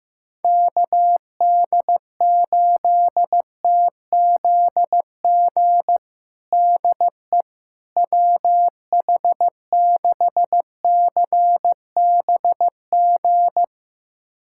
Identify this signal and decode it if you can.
Morse code, heard as a keyed tone, about 16 wpm, 700 Hz. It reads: KD8TZG DE WH6CBG
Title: USA Callsign DE Callsign - 6 Characters - 15wpm